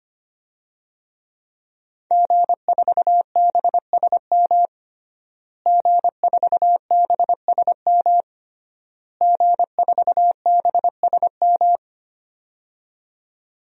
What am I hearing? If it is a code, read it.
Morse code, heard as a keyed tone, about 25 wpm, 700 Hz. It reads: G4BSM G4BSM G4BSM
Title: DXCC-100 Call Signs - 5 Characters - Repeated 3x - 25wpm